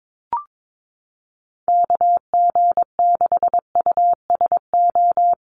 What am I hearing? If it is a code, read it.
Morse code, heard as a keyed tone, about 22 wpm, 700 Hz. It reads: KG6USO
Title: USA Call Signs Repeated 3x 22wpm